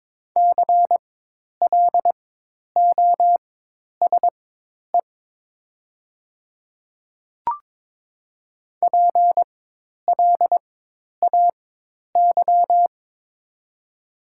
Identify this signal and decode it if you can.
Morse code, heard as a keyed tone, about 22 wpm, spaced as for 10 wpm, 700 Hz. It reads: CLOSE PLAY
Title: 4x Character Spacing - Top 300 Words - 22wpm